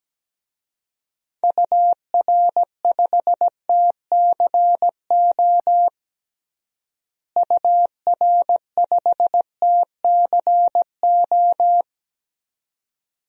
Morse code, heard as a keyed tone, about 17 wpm, 700 Hz. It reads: UR5TCO UR5TCO
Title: DXCC-100 Call Signs - 6 Characters - Repeated 2x - 17wpm